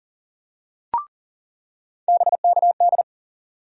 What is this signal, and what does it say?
Morse code, heard as a keyed tone, about 40 wpm, 700 Hz. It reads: BKD
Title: USA Call Sign Postfixes - 3 Letters 40wpm